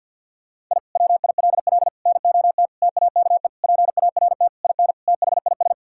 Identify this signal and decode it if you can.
Morse code, heard as a keyed tone, about 50 wpm, 700 Hz. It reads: I WILL NOT TAKE PART IN THIS
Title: Sentences from Top 1000 Words - Review Part 6 - 50wpm